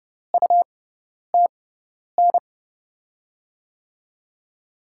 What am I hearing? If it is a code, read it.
Morse code, heard as a keyed tone, about 32 wpm, spaced as for 10 wpm, 700 Hz. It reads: UTN
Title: USA Call Signs - 6 Characters - 6x Character Spacing - 30wpm